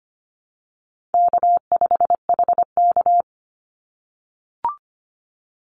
Morse code, heard as a keyed tone, about 25 wpm, 700 Hz. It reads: K5HK
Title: Easy Call Signs - 25wpm